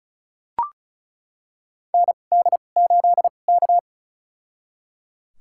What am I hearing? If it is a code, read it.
Morse code, heard as a keyed tone, about 36 wpm, spaced as for 25 wpm, 690 Hz. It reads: ND8K
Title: USA Call Signs - 4 Characters - 2x Character Spacing - 35wpm